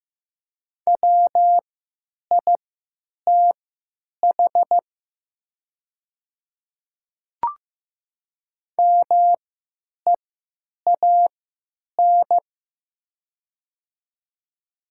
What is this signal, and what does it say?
Morse code, heard as a keyed tone, about 16 wpm, spaced as for 9 wpm, 700 Hz. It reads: WITH MEAN
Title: 3x Character Spacing - Top 200 Words - 15wpm